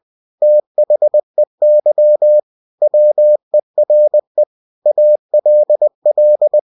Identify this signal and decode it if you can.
Morse code, heard as a keyed tone, about 20 wpm, 600 Hz. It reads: THEY WERE ALL
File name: cw-books / winnie-the-pooh / winnie-20wpm-page-0109